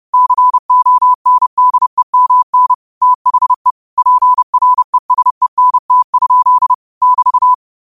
Morse code, heard as a keyed tone, about 30 wpm, 1000 Hz. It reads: CONDEMN THE PRESENT? <BT>